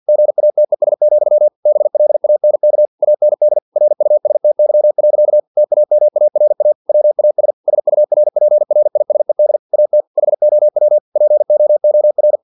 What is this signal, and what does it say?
Morse code, heard as a keyed tone, about 50 wpm, 600 Hz. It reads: GATES, BLANK AND RUSTY. TAMARA WAS SURPRISED AT HOW POOR
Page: Never 49wpm